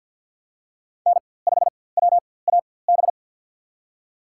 Morse code, heard as a keyed tone, about 51 wpm, spaced as for 23 wpm, 700 Hz. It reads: N5WAB
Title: USA Call Signs - 5 Characters - 4x Character Spacing - 50wpm